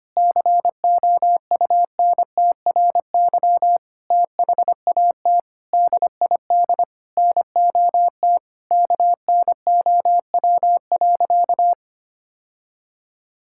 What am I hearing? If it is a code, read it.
Morse code, heard as a keyed tone, about 25 wpm, 700 Hz. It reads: COUNTRY THAT DID NOT KNOW.
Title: Sentences from Top 1600 Words 25wpm